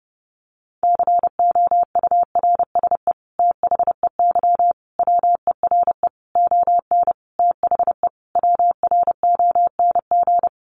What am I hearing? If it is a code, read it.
Morse code, heard as a keyed tone, about 30 wpm, 700 Hz. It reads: COURSE THEY WERE ON THE WRONG
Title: Sentences from Top 900 Words 30wpm